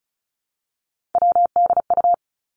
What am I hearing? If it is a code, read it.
Morse code, heard as a keyed tone, about 36 wpm, 700 Hz. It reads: WDU
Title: Three Challenging Letter-Number Combinations 35wpm